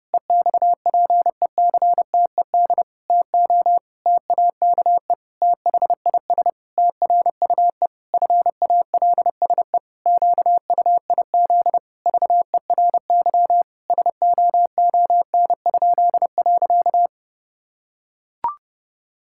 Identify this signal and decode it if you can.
Morse code, heard as a keyed tone, about 30 wpm, 700 Hz. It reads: EXPECTED TO TAKE THIS TRUE FALSE QUIZ VERY SOON?.
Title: Pangram Challenge v2 - 30wpm